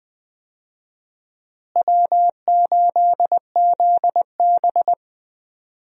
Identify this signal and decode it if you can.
Morse code, heard as a keyed tone, about 20 wpm, 700 Hz. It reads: W8ZB
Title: USA Call Signs - 4 Characters - Repeated 3x - 20wpm